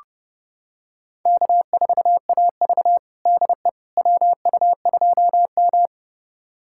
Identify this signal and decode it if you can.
Morse code, heard as a keyed tone, about 30 wpm, 700 Hz. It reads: K4AV DE WU2M
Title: USA Callsign DE Callsign - 4 Characters - 30wpm